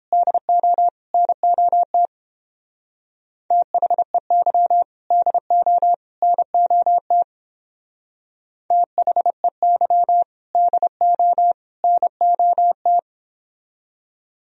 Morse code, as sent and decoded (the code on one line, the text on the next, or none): -.. --- / -. --- - / - .... -.-- / -.. --- / -. --- - / - .... -.-- / -.. --- / -. --- -
DO NOT THEY DO NOT THEY DO NOT